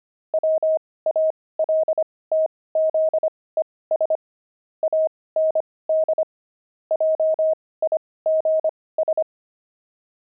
Morse code, heard as a keyed tone, about 25 wpm, spaced as for 18 wpm, 630 Hz. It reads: WALTZES AND JIGS
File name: pangram-fs-18-cs-25-heavy